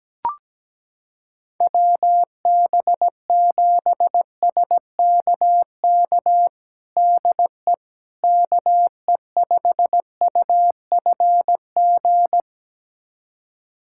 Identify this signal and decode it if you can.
Morse code, heard as a keyed tone, about 17 wpm, 700 Hz. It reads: WB7SKK DE KE5UFG
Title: USA Callsign DE Callsign - 6 Characters - 17wpm